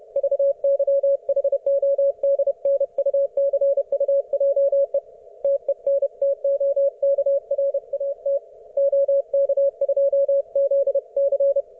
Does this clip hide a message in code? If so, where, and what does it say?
Yes: Morse code from 0.2 to 11.6 s, heard as a keyed tone, about 30 wpm, 560 Hz. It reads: VYHODNUCUJE TENTOKRAT OK2ZC